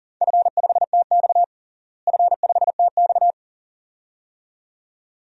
Morse code, heard as a keyed tone, about 37 wpm, 700 Hz. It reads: F5TX F5TX
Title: DXCC-100 Call Signs - 4 Characters - Repeated 2x - 40wpm